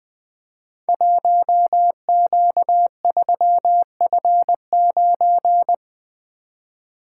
Morse code, heard as a keyed tone, about 20 wpm, 700 Hz. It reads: 1Q3F9
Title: Code Groups - 5 Letter:Numbers - 20wpm